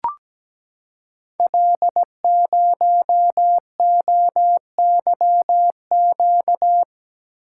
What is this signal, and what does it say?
Morse code, heard as a keyed tone, about 17 wpm, 700 Hz. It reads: L0OYQ
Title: Code Groups - 5 Letter:Numbers - 17wpm